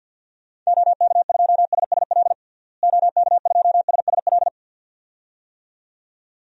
Morse code, heard as a keyed tone, about 48 wpm, 700 Hz. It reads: OK2SSL OK2SSL
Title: DXCC-100 Call Signs - 6 Characters - Repeated 2x - 50wpm